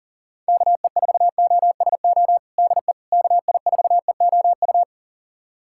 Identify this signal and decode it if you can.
Morse code, heard as a keyed tone, about 40 wpm, 700 Hz. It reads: KE4OSO DE KI4EOU
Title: USA Callsign DE Callsign - 6 Characters - 40wpm